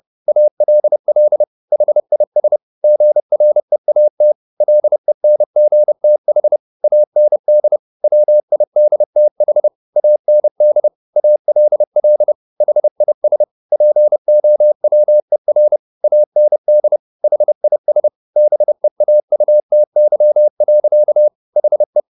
Morse code, heard as a keyed tone, about 30 wpm, 600 Hz. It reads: ALL HIS GREAT LENGTH AND WIDTH AND ALL HIS POWER AND HIS BEAUTY. HE